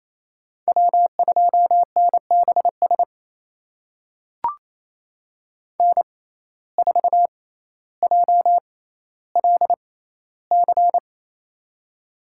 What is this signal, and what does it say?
Morse code, heard as a keyed tone, about 27 wpm, spaced as for 14 wpm, 700 Hz. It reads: W2NBS N4JLC
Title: USA Call Signs - 6x Character Spacing - 28wpm